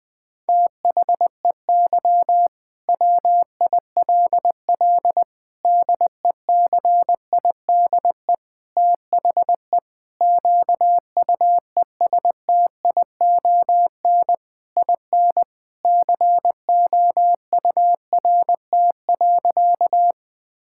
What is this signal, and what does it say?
Morse code, heard as a keyed tone, about 21 wpm, 700 Hz. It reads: THEY WILL DECIDE THE QUESTION IN COURT.
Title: Sentences from Top 1600 Words 20wpm